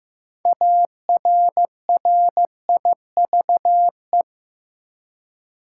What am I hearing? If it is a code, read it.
Morse code, heard as a keyed tone, about 15 wpm, 700 Hz. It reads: ARRIVE
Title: Six Letter Words - 15wpm